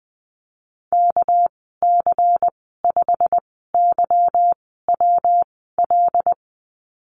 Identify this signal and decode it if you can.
Morse code, heard as a keyed tone, about 19 wpm, 700 Hz. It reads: K C 5 Y W L
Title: 207 Lesson 33g - Letter j - Call Signs 20wpm